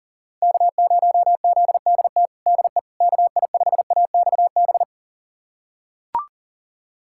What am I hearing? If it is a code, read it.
Morse code, heard as a keyed tone, about 40 wpm, 700 Hz. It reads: K0ZDT DE KI5AXB
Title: USA Callsign DE Callsign 40wpm